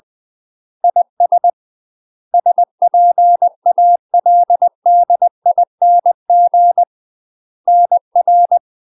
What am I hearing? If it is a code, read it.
Morse code, heard as a keyed tone, about 20 wpm, 700 Hz. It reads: IS SPALDING NR
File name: UK MASTER QSOs TRAINING / 20WPM / UK MASTER QSOs 0059